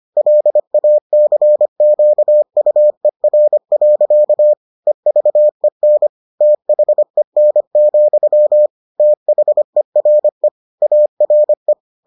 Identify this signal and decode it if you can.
Morse code, heard as a keyed tone, about 25 wpm, 600 Hz. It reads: LACQUER. EVEN THEN, THERE ARE